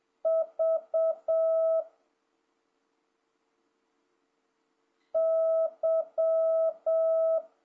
Morse code, heard as a keyed tone, about 7 wpm, 650 Hz. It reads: V Y
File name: jh8cbh / 2026cw